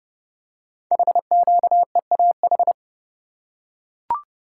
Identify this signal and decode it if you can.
Morse code, heard as a keyed tone, about 30 wpm, 700 Hz. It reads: HQEAH
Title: Code Groups - 5 Letters - 30wpm